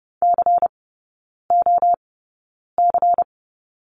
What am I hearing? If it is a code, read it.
Morse code, heard as a keyed tone, about 30 wpm, spaced as for 9 wpm, 700 Hz. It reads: COC
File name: Lesson 16e - Letter c - Three Characters 30wpm